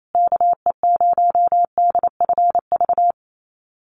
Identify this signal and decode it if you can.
Morse code, heard as a keyed tone, about 28 wpm, 700 Hz. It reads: KE0DFV
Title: USA Call Signs Repeated 3x 28wpm